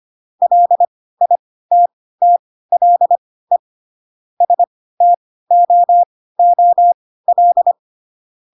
Morse code, heard as a keyed tone, about 25 wpm, spaced as for 16 wpm, 700 Hz. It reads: LITTLE STOOL
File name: Pinocchio / Pinocchio ch6 10 25 ik1hge